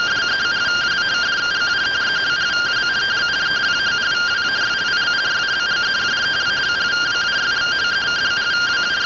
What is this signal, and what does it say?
Two-tone teleprinter signal, about 170 Hz shift.